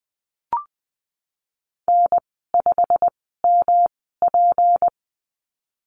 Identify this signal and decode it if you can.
Morse code, heard as a keyed tone, about 20 wpm, spaced as for 15 wpm, 700 Hz. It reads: N5MP
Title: 186 Lesson 30g - Letter b - Call Signs 20wpm